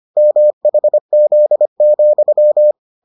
Morse code, heard as a keyed tone, about 25 wpm, 600 Hz. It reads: MHZ,